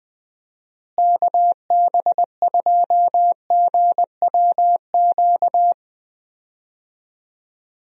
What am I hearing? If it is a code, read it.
Morse code, heard as a keyed tone, about 21 wpm, 700 Hz. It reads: KB2GWQ
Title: USA Call Signs - 6 Characters - Repeated 3x - 20wpm